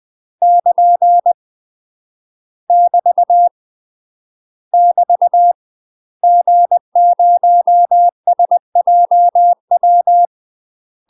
Morse code, heard as a keyed tone, about 20 wpm, 700 Hz. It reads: <KN> <BT> <BT> G0SJW
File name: UK MASTER QSOs TRAINING / 20WPM / UK MASTER QSOs 0032